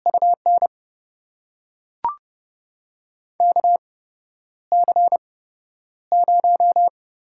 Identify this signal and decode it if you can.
Morse code, heard as a keyed tone, about 30 wpm, spaced as for 11 wpm, 700 Hz. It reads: UN KC0
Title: USA Call Signs - 6 Characters - 8x Character Spacing - 30wpm